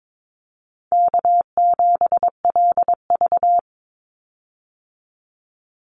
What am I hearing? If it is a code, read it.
Morse code, heard as a keyed tone, about 22 wpm, 700 Hz. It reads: K7LV
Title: Easy Call Signs - 22wpm